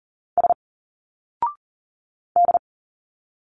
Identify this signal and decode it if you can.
Morse code, heard as a keyed tone, about 40 wpm, 700 Hz. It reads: S D
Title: Single Letters 40wpm